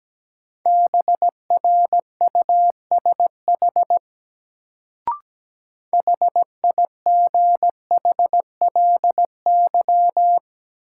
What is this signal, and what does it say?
Morse code, heard as a keyed tone, about 17 wpm, 700 Hz. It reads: BRUSH HIGHLY
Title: Sentences from Top 1100 Words - New Word Drill - 17wpm